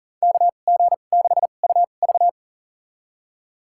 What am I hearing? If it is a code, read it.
Morse code, heard as a keyed tone, about 40 wpm, spaced as for 29 wpm, 700 Hz. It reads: KG6UV
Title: USA Call Signs - 5 Characters - 2x Character Spacing - 40wpm